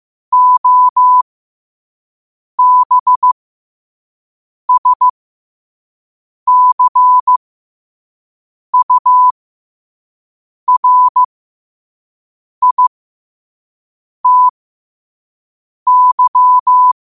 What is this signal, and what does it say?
Morse code, heard as a keyed tone, about 15 wpm, spaced as for 5 wpm, 1000 Hz. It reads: OBSCURITY
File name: qotdfiles / 5wpm